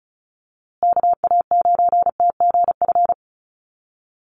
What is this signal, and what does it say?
Morse code, heard as a keyed tone, about 34 wpm, 700 Hz. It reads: KA9TGF
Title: USA Call Signs Repeated 2x 35wpm